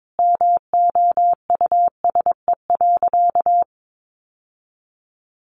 Morse code, heard as a keyed tone, about 22 wpm, 700 Hz. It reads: MOUSE.
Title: Sentences from Top 700 Words 22wpm